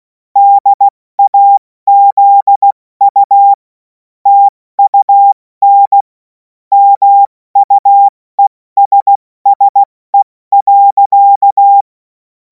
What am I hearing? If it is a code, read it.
Morse code, heard as a keyed tone, about 16 wpm, spaced as for 14 wpm, 800 Hz. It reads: DAZU TUN MUESSE.
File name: Morsekurs / Witze / Witz 03. März 2026 70 F80